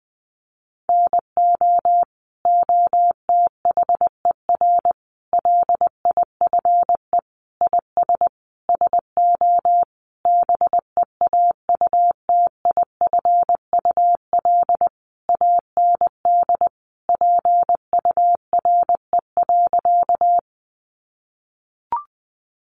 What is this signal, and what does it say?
Morse code, heard as a keyed tone, about 20 wpm, 700 Hz. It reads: NO OTHER LIFE IS SO BEAUTIFUL AND PURE.
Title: Sentences from Top 1500 Words 20wpm